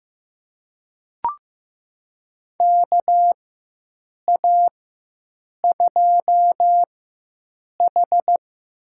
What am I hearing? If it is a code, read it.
Morse code, heard as a keyed tone, about 15 wpm, spaced as for 7 wpm, 700 Hz. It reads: KA2H
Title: USA Call Signs - 4x Character Spacing - 15wpm